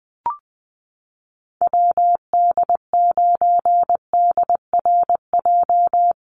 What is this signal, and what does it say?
Morse code, heard as a keyed tone, about 19 wpm, 700 Hz. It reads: WD9DRJ
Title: USA Call Signs Repeated 3x 20wpm